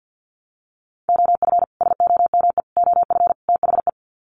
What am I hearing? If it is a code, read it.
Morse code, heard as a keyed tone, about 49 wpm, 700 Hz. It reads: OF SOME OF THE
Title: Sets of 4 Words 50wpm